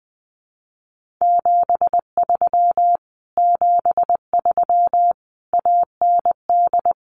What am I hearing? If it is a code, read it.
Morse code, heard as a keyed tone, about 20 wpm, 700 Hz. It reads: 73 73 AND
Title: Common QSO Phrases Encore 20wpm